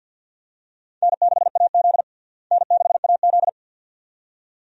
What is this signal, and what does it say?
Morse code, heard as a keyed tone, about 52 wpm, 700 Hz. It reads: N6AZ N6AZ
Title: USA Call Signs - 4 Characters - Repeated 2x - 50wpm